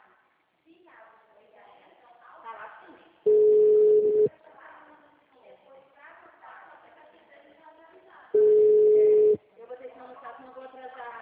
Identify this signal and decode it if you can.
Ringback tone (Europe, 425 Hz), with speech.
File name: out- -2016- - - .94111